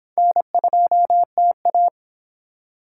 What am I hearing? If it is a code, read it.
Morse code, heard as a keyed tone, about 26 wpm, 700 Hz. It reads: N2TA